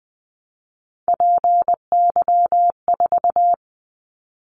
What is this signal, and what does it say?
Morse code, heard as a keyed tone, about 20 wpm, 700 Hz. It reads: PY4
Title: Callsign Prefixes for Top 50 DXCC - 20wpm